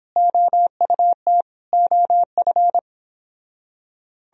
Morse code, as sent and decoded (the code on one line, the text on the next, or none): --- ..- - / --- ..-.
OUT OF